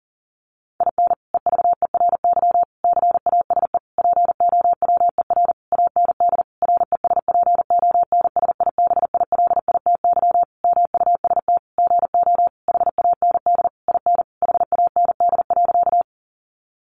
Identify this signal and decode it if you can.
Morse code, heard as a keyed tone, about 40 wpm, 700 Hz. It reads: IN EVERY CASE POWER AND RESPONSIBILITY MUST GO HAND IN HAND.
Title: Sentences from Top 1200 Words 40wpm